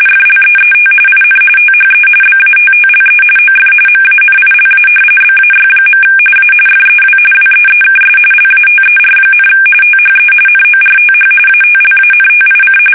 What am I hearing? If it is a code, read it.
RTTY, about 45.45 baud. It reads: FLEET IN THE BATTLE OF MANILA BAY. A FEW WEEKS LATER, THE NORTH ATLANTIC
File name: fsk850 / usnavy3